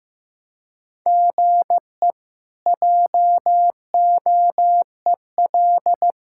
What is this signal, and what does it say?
Morse code, heard as a keyed tone, about 15 wpm, 700 Hz. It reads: GE JOEL
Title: K1USN SST QSO Practice - 15wpm